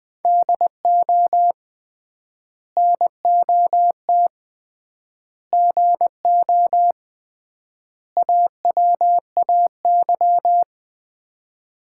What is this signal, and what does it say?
Morse code, heard as a keyed tone, about 20 wpm, 700 Hz. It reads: DO NOT GO AWAY
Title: 3x Word-Spacing - Sets of 4 Words - 20wpm